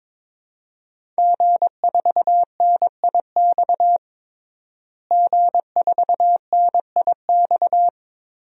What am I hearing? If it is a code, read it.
Morse code, heard as a keyed tone, about 22 wpm, 700 Hz. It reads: G4NIX G4NIX
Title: DXCC-100 Call Signs - 5 Characters - Repeated 3x - 22wpm